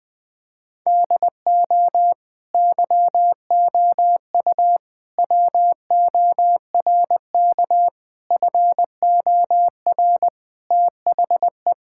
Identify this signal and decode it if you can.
Morse code, heard as a keyed tone, about 20 wpm, 700 Hz. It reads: DO YOU WORK FOR THE
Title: Sentences from Top 1000 Words - Review Part 5 - 20wpm